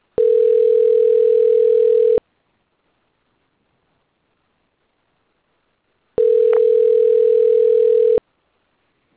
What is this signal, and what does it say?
Ringback tone (North America), 440 + 480 Hz.